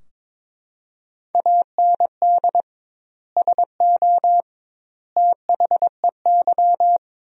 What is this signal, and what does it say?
Morse code, heard as a keyed tone, about 22 wpm, 700 Hz. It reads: AND SO THEY
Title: 2x Word-Spacing - Sentences from Top 400 Words - 22wpm